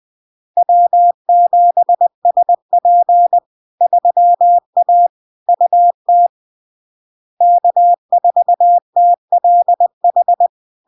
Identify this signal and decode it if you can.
Morse code, heard as a keyed tone, about 20 wpm, 700 Hz. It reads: W7SP 3A UT K4TLH
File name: LARK-FD-2022-Log-in-CW-20wpm-700-Hz